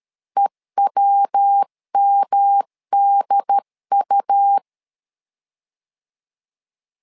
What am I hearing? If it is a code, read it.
Morse code, heard as a keyed tone, about 13 wpm, 800 Hz. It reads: EWMDU
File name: LEZIONI / 3 MWRO / 3 MWR 12wpm 1711